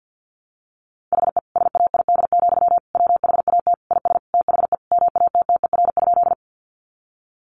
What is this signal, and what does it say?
Morse code, heard as a keyed tone, about 50 wpm, 700 Hz. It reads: HE SAID, WHAT IS THE MATTER?
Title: Sentences from Top 700 Words 50wpm